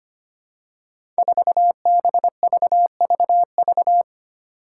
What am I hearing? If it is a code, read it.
Morse code, heard as a keyed tone, about 25 wpm, 700 Hz. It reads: 4BVVV
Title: Five Challenging Letter-Number Combinations 25wpm